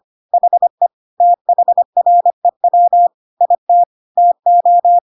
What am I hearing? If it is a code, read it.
Morse code, heard as a keyed tone, about 25 wpm, 700 Hz. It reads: HE THREW IT TO
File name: Pinocchio / Pinocchio ch1 25 25 ik1hge